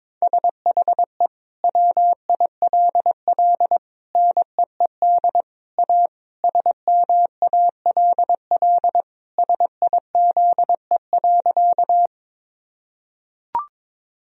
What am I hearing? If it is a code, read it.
Morse code, heard as a keyed tone, about 21 wpm, 700 Hz. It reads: SHE WILL NEED A SMALL SIZE.
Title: Sentences from Top 1000 Words - Review Part 2 - 22wpm